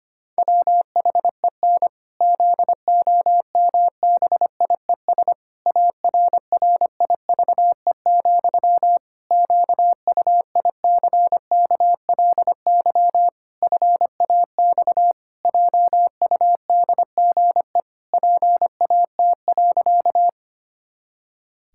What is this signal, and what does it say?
Morse code, heard as a keyed tone, about 25 wpm, 700 Hz. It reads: WHEN ZOMBIES ARRIVE, QUICKLY FAX JUDGE PAT.
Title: Pangram Challenge v2 - 25wpm